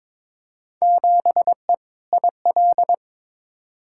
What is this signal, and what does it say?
Morse code, heard as a keyed tone, about 22 wpm, 700 Hz. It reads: 7E IL
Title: Field Day Exchanges Encore - 22wpm